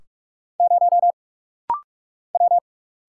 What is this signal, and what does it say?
Morse code, heard as a keyed tone, about 46 wpm, 700 Hz. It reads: 0 W
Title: Single Letter-Number Alt - Rapid-Fire 45wpm